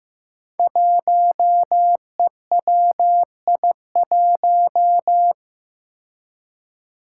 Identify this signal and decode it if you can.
Morse code, heard as a keyed tone, about 15 wpm, 700 Hz. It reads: 1EWI1
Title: Five Challenging Letter-Number Combinations 15wpm